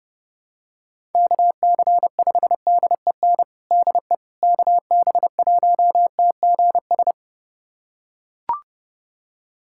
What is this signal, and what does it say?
Morse code, heard as a keyed tone, about 30 wpm, 700 Hz. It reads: KC5DEN DE KB1TGS
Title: USA Callsign DE Callsign - 6 Characters - 30wpm